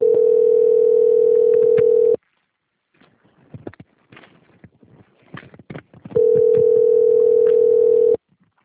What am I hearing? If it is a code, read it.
Ringback tone (North America), 440 + 480 Hz.
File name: out- - - - - .32869